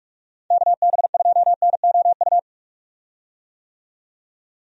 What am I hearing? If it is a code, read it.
Morse code, heard as a keyed tone, about 44 wpm, 700 Hz. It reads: KB2NOU
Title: USA Call Signs - 6 Characters - 45wpm